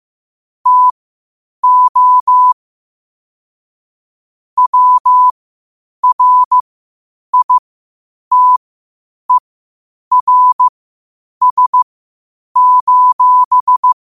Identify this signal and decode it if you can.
Morse code, heard as a keyed tone, about 15 wpm, spaced as for 8 wpm, 1000 Hz. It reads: TO WRITERS:
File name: qotdfiles / 8wpm